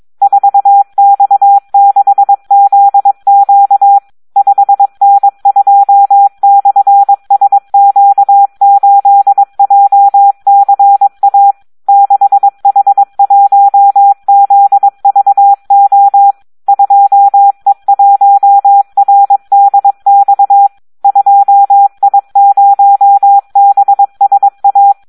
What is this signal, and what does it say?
Morse code, heard as a keyed tone, about 22 wpm, 800 Hz. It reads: 4X6ZQ 5N2/SQ8JCA 6H1ZVO 2E1RDX 2I0BSA
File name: Audio practica CW / SD 1 22